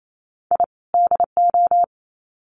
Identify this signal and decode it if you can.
Morse code, heard as a keyed tone, about 28 wpm, 700 Hz. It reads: I DO
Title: Sets of 2 Words 28wpm